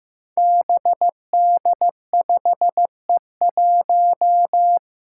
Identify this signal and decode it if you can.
Morse code, heard as a keyed tone, about 15 wpm, 700 Hz. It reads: BD5E1
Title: Five Challenging Letter-Number Combinations 15wpm